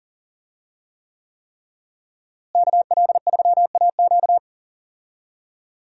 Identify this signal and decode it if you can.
Morse code, heard as a keyed tone, about 40 wpm, 700 Hz. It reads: KL3AQ
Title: USA Call Signs - 5 Characters - Repeated 3x - 40wpm